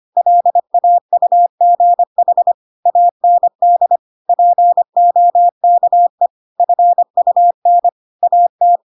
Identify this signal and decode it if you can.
Morse code, heard as a keyed tone, about 25 wpm, 700 Hz. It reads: LAUGH AND POKE FUN AT
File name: Pinocchio / Pinocchio ch3 25 25 ik1hge